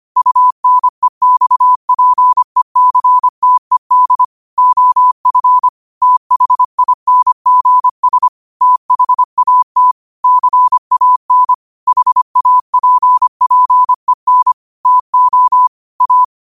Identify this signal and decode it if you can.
Morse code, heard as a keyed tone, about 25 wpm, 1000 Hz. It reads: UNEXPECTED OF THINGS THAT CAN HAPPEN TO A
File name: qotdfiles / 25wpm